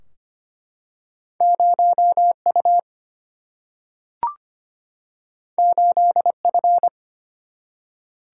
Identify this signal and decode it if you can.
Morse code, heard as a keyed tone, about 25 wpm, 700 Hz. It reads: 0 U 8 F
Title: Single-Letter Single-Number 25wpm